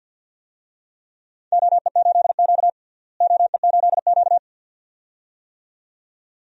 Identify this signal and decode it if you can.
Morse code, heard as a keyed tone, about 53 wpm, 700 Hz. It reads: OE8Q OE8Q
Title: DXCC-100 Call Signs - 4 Characters - Repeated 2x - 50wpm